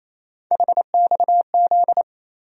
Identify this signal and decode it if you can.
Morse code, heard as a keyed tone, about 28 wpm, 700 Hz. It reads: HXZ
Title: USA Call Sign Postfixes - 3 Letters 28wpm